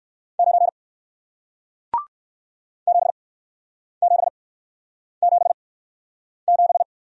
Morse code, heard as a keyed tone, about 60 wpm, spaced as for 9 wpm, 700 Hz. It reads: 9 7777